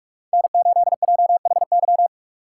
Morse code, heard as a keyed tone, about 43 wpm, 700 Hz. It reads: N8JHY